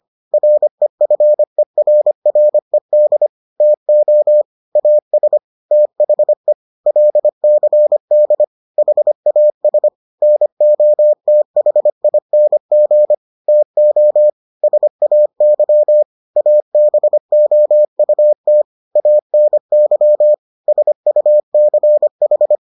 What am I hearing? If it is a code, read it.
Morse code, heard as a keyed tone, about 25 wpm, 600 Hz. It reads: REFERRED TO AS THE LCD HAS NOTHING TO SAY ABOUT ANY SUCH